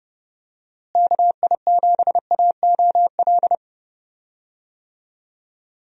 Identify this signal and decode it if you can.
Morse code, heard as a keyed tone, about 30 wpm, 700 Hz. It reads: KI7AOL
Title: USA Call Signs - 6 Characters - Repeated 3x - 30wpm